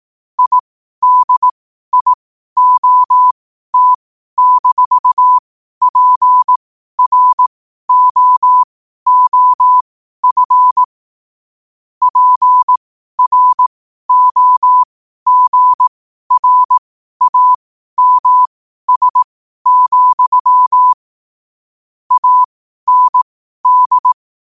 Morse code, heard as a keyed tone, about 18 wpm, spaced as for 12 wpm, 1000 Hz. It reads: IDIOT-PROOF PROGRAMS, AND
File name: qotdfiles / 12wpm